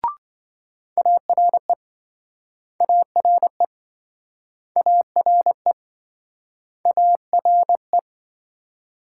Morse code, heard as a keyed tone, about 27 wpm, 700 Hz. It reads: ARE ARE ARE ARE